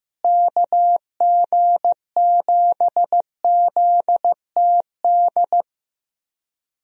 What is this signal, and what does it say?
Morse code, heard as a keyed tone, about 15 wpm, 700 Hz. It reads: KG7ZTD
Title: USA Call Signs - 6 Characters - Repeated 3x - 15wpm